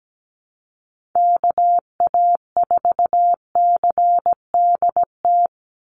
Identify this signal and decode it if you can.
Morse code, heard as a keyed tone, about 17 wpm, 700 Hz. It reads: KA4CDT
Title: USA Call Signs 17wpm